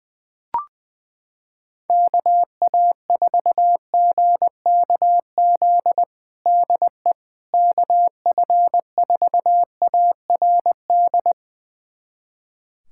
Morse code, heard as a keyed tone, about 20 wpm, 700 Hz. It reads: KA4GKZ DE KF4ARD
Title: USA Callsign DE Callsign - 6 Characters - 20wpm